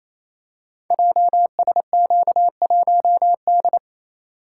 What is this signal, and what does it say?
Morse code, heard as a keyed tone, about 28 wpm, 700 Hz. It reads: JSQ1D
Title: Code Groups - 5 Letter:Numbers - 28wpm